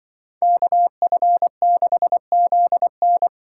Morse code, heard as a keyed tone, about 24 wpm, 700 Hz. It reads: KF6ZN
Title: Speed Racing - USA Call Signs 20wpm